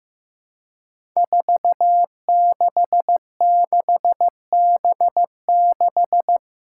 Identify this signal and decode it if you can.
Morse code, heard as a keyed tone, about 15 wpm, 700 Hz. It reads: 466B6
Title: Five Challenging Letter-Number Combinations 15wpm